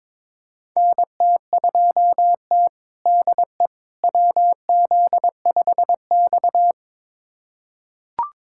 Morse code, heard as a keyed tone, about 22 wpm, 700 Hz. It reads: NT2T DE WZ5X
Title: USA Callsign DE Callsign - 4 Characters - 22wpm